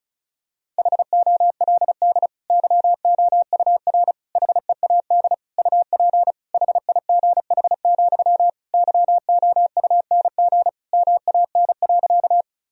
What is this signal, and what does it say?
Morse code, heard as a keyed tone, about 34 wpm, 700 Hz. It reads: HOLD YOUR HEAD UP HIGH, YOUNG MAN.
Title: Sentences from Top 1000 Words - Review Part 2 - 35wpm